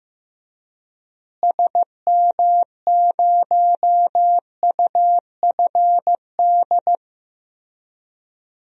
Morse code, heard as a keyed tone, about 15 wpm, 700 Hz. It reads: SM0UFD
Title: DXCC-100 Call Signs - 6 Characters - Repeated 3x - 15wpm